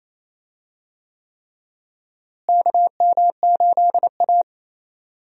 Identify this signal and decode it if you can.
Morse code, heard as a keyed tone, about 28 wpm, 700 Hz. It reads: KM8A
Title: USA Call Signs - 4 Characters - Repeated 3x - 28wpm